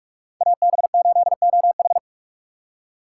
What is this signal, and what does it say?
Morse code, heard as a keyed tone, about 44 wpm, 700 Hz. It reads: AB8OH